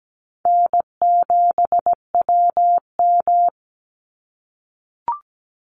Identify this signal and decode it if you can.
Morse code, heard as a keyed tone, about 17 wpm, 700 Hz. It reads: N7WM
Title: NA SOTA Chaser Callsigns 17wpm